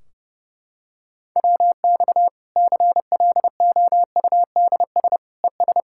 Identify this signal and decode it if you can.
Morse code, heard as a keyed tone, about 32 wpm, 700 Hz. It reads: WX CLOUDS ES
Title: Practice QSO - Exchange 2 - Info - 30wpm